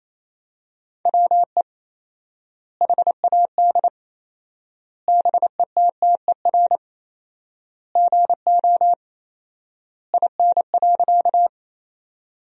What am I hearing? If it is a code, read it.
Morse code, heard as a keyed tone, about 28 wpm, 700 Hz. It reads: WE HAD BETTER GO IN.
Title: 4x Word-Spacing - Sentences from Top 400 Words - 28wpm